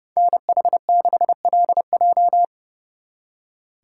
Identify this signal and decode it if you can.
Morse code, heard as a keyed tone, about 36 wpm, 700 Hz. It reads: NH6LJ
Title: Speed Racing - USA 2x2 Call Signs 30wpm